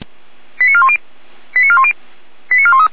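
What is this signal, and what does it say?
Five-tone selective call (ZVEI), heard as quick runs of five tones, tones of about 70 ms. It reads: 89410 89410 89413
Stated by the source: ZVEI five-tone selective call received by SDR.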